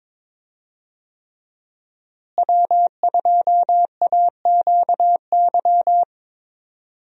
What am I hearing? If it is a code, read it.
Morse code, heard as a keyed tone, about 22 wpm, 700 Hz. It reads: W2AQY
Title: USA Call Signs - 5 Characters - Repeated 3x - 22wpm